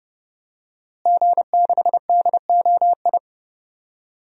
Morse code, heard as a keyed tone, about 30 wpm, 700 Hz. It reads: G6DOI
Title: DXCC-100 Call Signs - 5 Characters - Repeated 3x - 30wpm